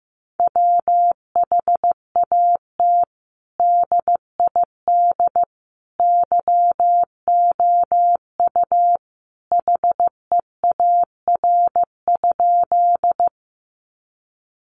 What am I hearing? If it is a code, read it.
Morse code, heard as a keyed tone, about 15 wpm, 700 Hz. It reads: WHAT DID YOU HEAR?
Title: Sentences from Top 300 Words 15wpm